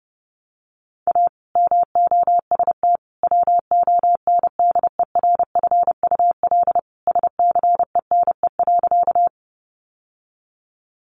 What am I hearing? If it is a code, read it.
Morse code, heard as a keyed tone, about 30 wpm, 700 Hz. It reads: A MOST WONDERFUL SCENE.
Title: Sentences from Top 1100 Words 30wpm